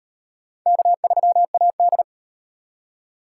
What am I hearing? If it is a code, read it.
Morse code, heard as a keyed tone, about 38 wpm, 700 Hz. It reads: K3AD